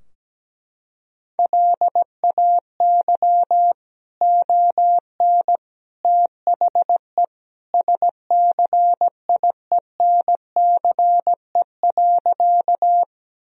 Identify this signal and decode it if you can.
Morse code, heard as a keyed tone, about 17 wpm, 700 Hz. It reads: LAY ON THE SCIENCE.
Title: Sentences from Top 1000 Words - Review Part 2 - 17wpm